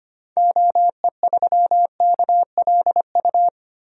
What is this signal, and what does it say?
Morse code, heard as a keyed tone, about 25 wpm, 700 Hz. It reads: OE3KLU
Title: DXCC-100 Call Signs - 6 Characters - Repeated 3x - 25wpm